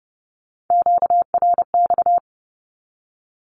Morse code, heard as a keyed tone, about 30 wpm, 700 Hz. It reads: QRX
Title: Q-Codes 30wpm